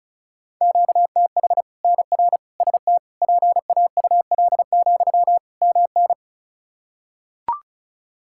Morse code, heard as a keyed tone, about 36 wpm, 700 Hz. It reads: QTH NR ST PAUL, MN
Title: K1USN SST QSO Practice 35wpm